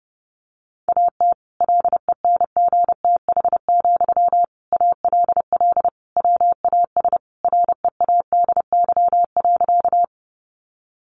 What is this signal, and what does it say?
Morse code, heard as a keyed tone, about 30 wpm, 700 Hz. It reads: AT LENGTH, ALL WAS READY.
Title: Sentences from Top 1000 Words 30wpm